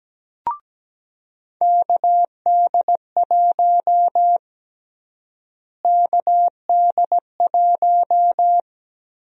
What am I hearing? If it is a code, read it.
Morse code, heard as a keyed tone, about 17 wpm, 700 Hz. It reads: KD1 KD1
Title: USA Call Sign Prefixes - Repeated 2x - 17wpm